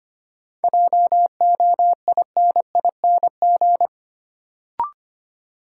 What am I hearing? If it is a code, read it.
Morse code, heard as a keyed tone, about 25 wpm, 700 Hz. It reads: JOINING
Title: Sets of 5 Words - New Word Drill - 25wpm